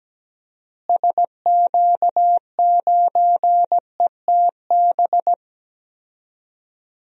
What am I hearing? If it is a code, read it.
Morse code, heard as a keyed tone, about 17 wpm, 700 Hz. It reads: SQ9ETB
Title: DXCC-100 Call Signs - 6 Characters - Repeated 2x - 17wpm